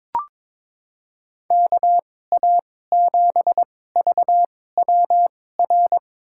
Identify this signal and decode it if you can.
Morse code, heard as a keyed tone, about 22 wpm, spaced as for 16 wpm, 700 Hz. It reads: KA7VWR
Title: USA Call Signs - 6 Characters - 2x Character Spacing - 22wpm